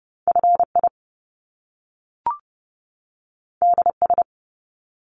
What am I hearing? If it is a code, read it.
Morse code, heard as a keyed tone, about 30 wpm, 700 Hz. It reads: FI DS
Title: Top Two Letter Combinations in Words 30wpm